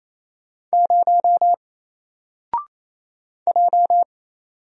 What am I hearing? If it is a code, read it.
Morse code, heard as a keyed tone, about 28 wpm, 700 Hz. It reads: 0 J
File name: Single-Letter-Number-28wpm